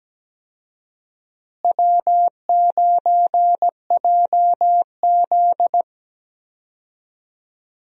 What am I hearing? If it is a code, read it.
Morse code, heard as a keyed tone, about 17 wpm, 700 Hz. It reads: W9JZ
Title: USA Call Signs - 4 Characters - Repeated 2x - 17wpm